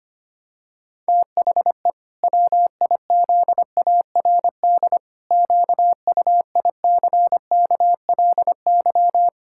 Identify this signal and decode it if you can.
Morse code, heard as a keyed tone, about 25 wpm, 700 Hz. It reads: THE WIZARD QUICKLY
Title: Pangram Challenge v2 - 25wpm